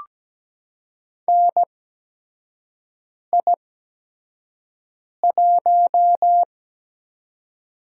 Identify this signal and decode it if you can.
Morse code, heard as a keyed tone, about 17 wpm, spaced as for 5 wpm, 700 Hz. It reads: NI1
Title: USA Call Signs - 4 Characters - 8x Character Spacing - 17wpm